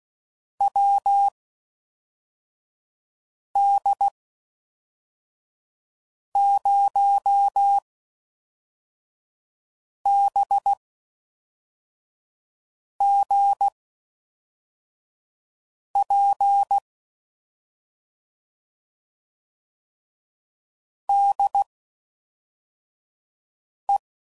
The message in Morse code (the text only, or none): WD0BGP DE